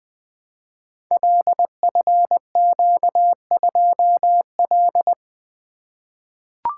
Morse code, heard as a keyed tone, about 20 wpm, 700 Hz. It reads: LFQ2L
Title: Code Groups - 5 Letter:Numbers - 20wpm